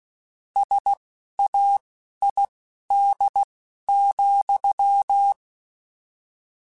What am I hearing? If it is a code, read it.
Morse code, heard as a keyed tone, about 16 wpm, spaced as for 11 wpm, 800 Hz. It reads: SAID,
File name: Technical Subjects / Technical Subjects-mp3s / OR Gate @10